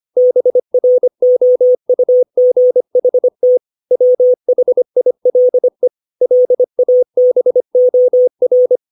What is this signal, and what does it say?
Morse code, heard as a keyed tone, about 25 wpm, 500 Hz. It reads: BROUGHT WHILE LABOR
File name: 632 words / 632 Words 25WPM